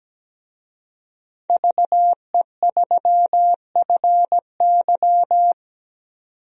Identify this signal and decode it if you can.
Morse code, heard as a keyed tone, about 17 wpm, 700 Hz. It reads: VE3FY
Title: DXCC-100 Call Signs - 5 Characters - 17wpm